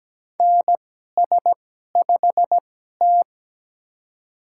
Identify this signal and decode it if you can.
Morse code, heard as a keyed tone, about 17 wpm, spaced as for 12 wpm, 700 Hz. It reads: NS5T
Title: USA Call Signs - 4 Characters - 2x Character Spacing - 17wpm